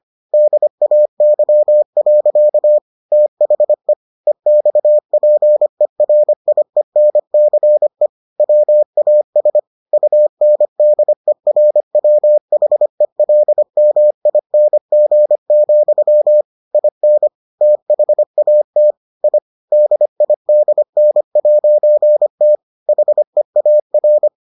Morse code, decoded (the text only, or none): DAY. THE EXPERIENCE WAS UNDERWHELMING, IN THAT I DIDN'T HEAR